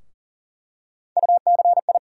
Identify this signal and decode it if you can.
Morse code, heard as a keyed tone, about 40 wpm, 700 Hz. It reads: UCI
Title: USA Call Sign Postfixes - 3 Letters 40wpm